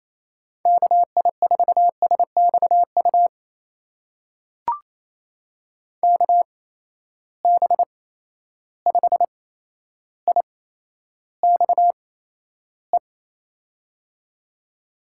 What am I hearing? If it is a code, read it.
Morse code, heard as a keyed tone, about 28 wpm, spaced as for 12 wpm, 700 Hz. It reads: KI4SXU KB5IXE
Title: USA Call Signs - 6 Characters - 8x Character Spacing - 28wpm